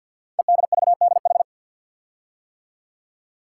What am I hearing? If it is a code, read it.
Morse code, heard as a keyed tone, about 53 wpm, 700 Hz. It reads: EDVDH
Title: Code Groups - 5 Letters - 50wpm